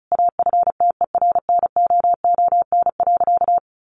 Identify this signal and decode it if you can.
Morse code, heard as a keyed tone, about 34 wpm, 700 Hz. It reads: AFTERNOON.
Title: Sentences from Top 1500 Words 35wpm